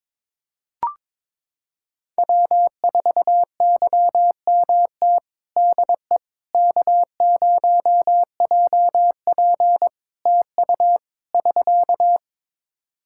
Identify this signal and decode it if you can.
Morse code, heard as a keyed tone, about 22 wpm, 700 Hz. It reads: W4YMT DE K0JP TU <SK>
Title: Practice QSO - Exchange 4 - Ending - 22wpm